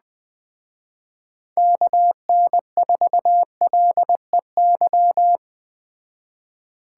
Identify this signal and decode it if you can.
Morse code, heard as a keyed tone, about 19 wpm, 700 Hz. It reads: KN4LEY